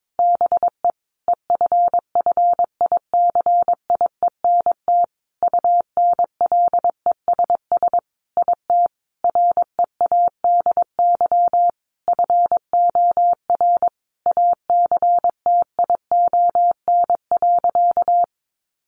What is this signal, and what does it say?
Morse code, heard as a keyed tone, about 22 wpm, 700 Hz. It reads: BE EFFICIENT UNLESS IT READY FOR ACTION.
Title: Sentences from Top 1300 Words 22wpm